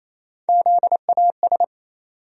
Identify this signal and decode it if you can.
Morse code, heard as a keyed tone, about 28 wpm, 700 Hz. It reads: ZAS
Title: USA Call Sign Postfixes - 3 Letters 28wpm